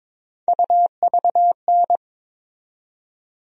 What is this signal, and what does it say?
Morse code, heard as a keyed tone, about 22 wpm, 700 Hz. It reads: UVN
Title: USA Call Sign Postfixes - 3 Letters 22wpm